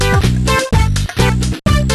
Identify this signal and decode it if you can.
There is no code.